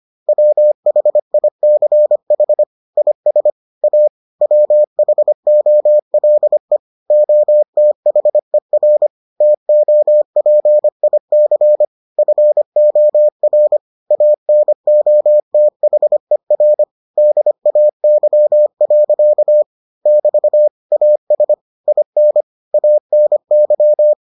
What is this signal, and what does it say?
Morse code, heard as a keyed tone, about 25 wpm, 600 Hz. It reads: WHICH IS A WHOLE OTHER TOPIC FOR ANOTHER DAY. <BT> AS IN ANY